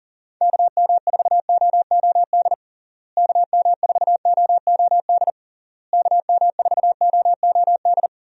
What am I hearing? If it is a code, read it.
Morse code, heard as a keyed tone, about 40 wpm, 700 Hz. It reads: KM4OOD KM4OOD KM4OOD
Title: USA Call Signs - 6 Characters - Repeated 3x - 40wpm